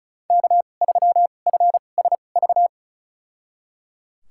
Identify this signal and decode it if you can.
Morse code, heard as a keyed tone, about 34 wpm, spaced as for 25 wpm, 700 Hz. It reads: K3FSV
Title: USA Call Signs - 2x Character Spacing - 35wpm